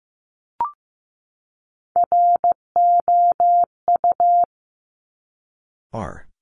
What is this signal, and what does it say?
Morse code, heard as a keyed tone, about 15 wpm, 700 Hz. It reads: ROU
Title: Top Three Letter Combinations in Words 15wpm